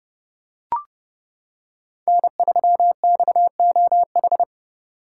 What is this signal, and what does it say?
Morse code, heard as a keyed tone, about 31 wpm, 700 Hz. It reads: N3XOH